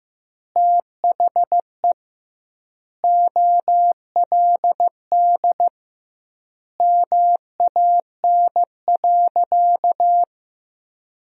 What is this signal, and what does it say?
Morse code, heard as a keyed tone, about 15 wpm, 700 Hz. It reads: THE OLD MAN.
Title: 2x Word-Spacing - Sentences from Top 500 Words - 15wpm